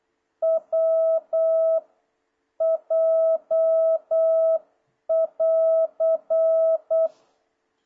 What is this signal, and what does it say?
Morse code, heard as a keyed tone, about 8 wpm, 660 Hz. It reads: W J <AR>
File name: jh8cbh / 2026cw